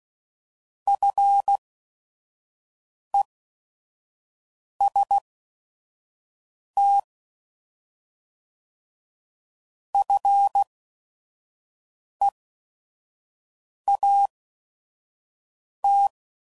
Morse code, heard as a keyed tone, about 16 wpm, spaced as for 5 wpm, 800 Hz. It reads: FEST FEAT